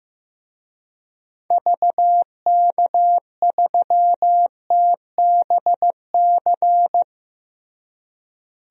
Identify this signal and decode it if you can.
Morse code, heard as a keyed tone, about 15 wpm, 700 Hz. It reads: VK3TBC
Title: DXCC-100 Call Signs - 6 Characters - Repeated 3x - 15wpm